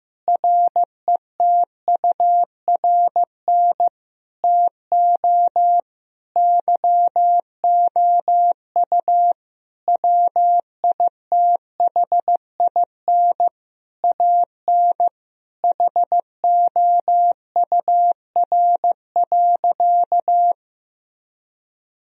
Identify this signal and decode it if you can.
Morse code, heard as a keyed tone, about 15 wpm, 700 Hz. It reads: RETURN TO YOU WITHIN AN HOUR.
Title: Sentences from Top 1000 Words - Review Part 4 - 15wpm